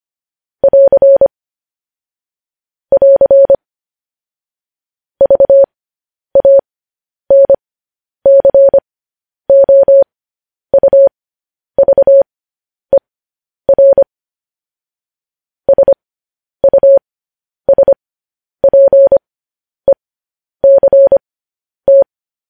Morse code, heard as a keyed tone, about 25 wpm, spaced as for 10 wpm, 550 Hz. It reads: <AR> <AR> VANCOUVER SUSPECT
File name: podcast / News headlines-1x-25x10